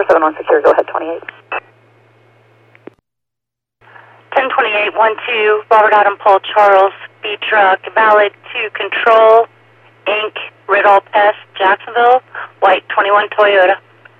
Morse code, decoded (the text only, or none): none